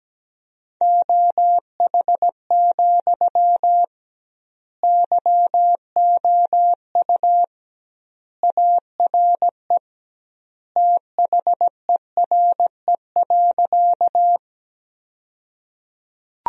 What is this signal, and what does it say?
Morse code, heard as a keyed tone, about 17 wpm, 700 Hz. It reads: OH, YOU ARE THERE.
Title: 2x Word-Spacing - Sentences from Top 500 Words - 17wpm